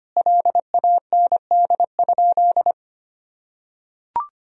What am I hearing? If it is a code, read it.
Morse code, heard as a keyed tone, about 25 wpm, 700 Hz. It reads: LAND?
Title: Sentences from Top 1000 Words - Review Part 2 - 25wpm